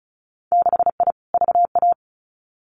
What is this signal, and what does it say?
Morse code, heard as a keyed tone, about 34 wpm, 700 Hz. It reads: 6I VA
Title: Winter Field Day Exchanges Encore 35wpm